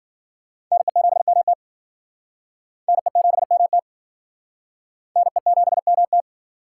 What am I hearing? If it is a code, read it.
Morse code, heard as a keyed tone, about 50 wpm, 700 Hz. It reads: NE7MT NE7MT NE7MT
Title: Speed Racing - USA 2x2 Call Signs 40wpm